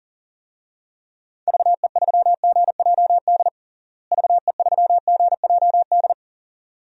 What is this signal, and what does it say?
Morse code, heard as a keyed tone, about 40 wpm, 700 Hz. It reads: VE3GJD VE3GJD
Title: DXCC-100 Call Signs - 6 Characters - Repeated 2x - 40wpm